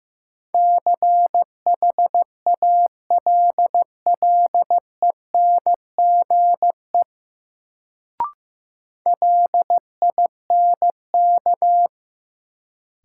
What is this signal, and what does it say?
Morse code, heard as a keyed tone, about 15 wpm, 700 Hz. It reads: CHALLENGE LINK
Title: Sentences from Top 1000 Words - Review Part 7 - New Word Drill - 15wpm